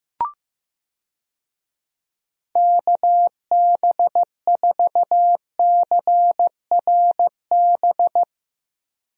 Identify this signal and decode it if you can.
Morse code, heard as a keyed tone, about 15 wpm, 700 Hz. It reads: KB4CRB
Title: USA Call Signs - 6 Characters - Repeated 2x - 15wpm